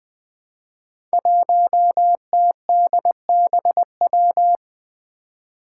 Morse code, heard as a keyed tone, about 20 wpm, 700 Hz. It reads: 1TDBW
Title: Five Challenging Letter-Number Combinations 20wpm